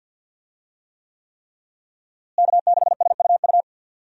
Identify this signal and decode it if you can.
Morse code, heard as a keyed tone, about 52 wpm, 700 Hz. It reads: K6SUU
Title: USA Call Signs - 5 Characters - Repeated 3x - 50wpm